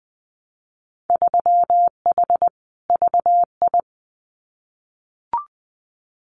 Morse code, heard as a keyed tone, about 20 wpm, 700 Hz. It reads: 3H VI
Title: Winter Field Day Exchanges Encore 20wpm